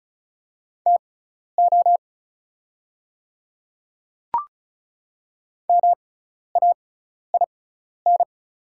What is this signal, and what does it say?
Morse code, heard as a keyed tone, about 36 wpm, spaced as for 12 wpm, 690 Hz. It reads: TO MAIN
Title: 6x Character Spacing - Top 300 Words - 35wpm